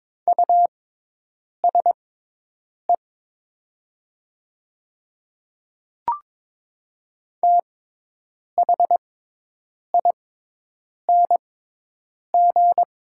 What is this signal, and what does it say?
Morse code, heard as a keyed tone, about 22 wpm, spaced as for 8 wpm, 700 Hz. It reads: USE THING
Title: 6x Character Spacing - Top 100 Words - 22wpm